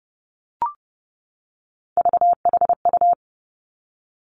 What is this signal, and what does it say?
Morse code, heard as a keyed tone, about 30 wpm, 700 Hz. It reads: VHU
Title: Three Challenging Letter-Number Combinations 30wpm